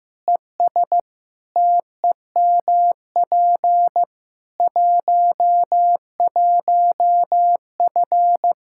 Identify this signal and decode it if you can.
Morse code, heard as a keyed tone, about 15 wpm, 700 Hz. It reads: ES TEMP 11F
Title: Practice QSO - Exchange 2 - Info - 15wpm